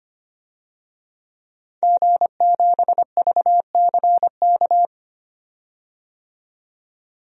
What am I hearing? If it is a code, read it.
Morse code, heard as a keyed tone, about 25 wpm, 700 Hz. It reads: G7VCK
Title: DXCC-100 Call Signs - 5 Characters - 25wpm